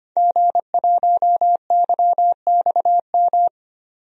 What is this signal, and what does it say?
Morse code, heard as a keyed tone, about 25 wpm, 700 Hz. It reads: G1YXM
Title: DXCC-100 Call Signs - 5 Characters - 25wpm